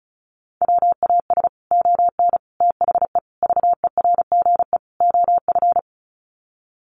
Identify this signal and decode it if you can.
Morse code, heard as a keyed tone, about 34 wpm, 700 Hz. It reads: WAS ON THE VERGE OF
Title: Sets of 5 Words Encore 35wpm